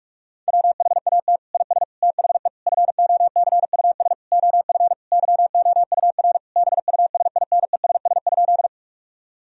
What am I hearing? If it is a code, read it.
Morse code, heard as a keyed tone, about 46 wpm, 700 Hz. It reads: WHAT IS THE FOCUS OF YOUR BUSINESS?
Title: Sentences from Top 1000 Words - Review Part 4 - 45wpm